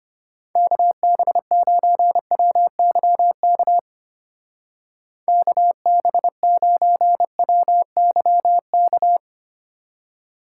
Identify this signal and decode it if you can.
Morse code, heard as a keyed tone, about 32 wpm, 700 Hz. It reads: KB9WYK KB9WYK